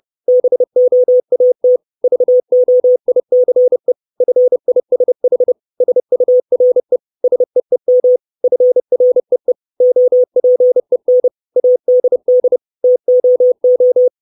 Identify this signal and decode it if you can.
Morse code, heard as a keyed tone, about 30 wpm, 500 Hz. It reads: BOAT VOICE FISH SURE SEEM FREE OPEN ADD TOO
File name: 632 Words 30WPM